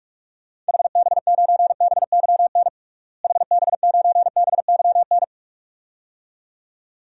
Morse code, heard as a keyed tone, about 42 wpm, 700 Hz. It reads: HB9BYN HB9BYN
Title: DXCC-100 Call Signs - 6 Characters - Repeated 2x - 45wpm